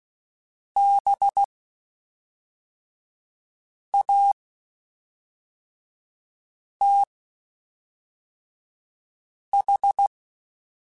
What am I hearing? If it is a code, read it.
Morse code, heard as a keyed tone, about 16 wpm, spaced as for 3 wpm, 800 Hz. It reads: BATH
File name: General / General-mp3s / Houses @5